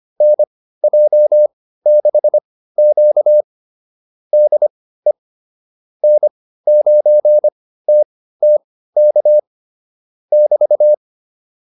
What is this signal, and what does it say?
Morse code, heard as a keyed tone, about 25 wpm, spaced as for 15 wpm, 600 Hz. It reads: NJ6Q DE N9TTK <BT>